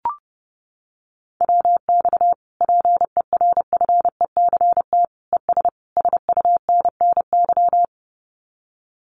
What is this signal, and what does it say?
Morse code, heard as a keyed tone, about 30 wpm, 700 Hz. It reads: WX PERFECT ES SUNNY
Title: Common QSO Phrases Encore 30wpm